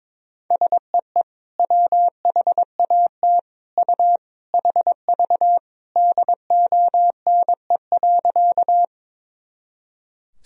Morse code, heard as a keyed tone, about 22 wpm, 700 Hz. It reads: SEE WHAT U HV DONE.
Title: Sentences from Top 500 Words and Abbreviations 22wpm